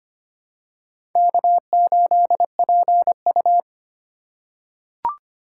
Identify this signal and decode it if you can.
Morse code, heard as a keyed tone, about 25 wpm, 700 Hz. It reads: K8PU
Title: USA Call Signs - 2x Character Spacing - 25wpm